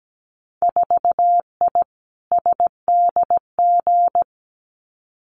Morse code, heard as a keyed tone, about 17 wpm, 700 Hz. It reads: 4I SDG
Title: Winter Field Day Exchanges Encore 17wpm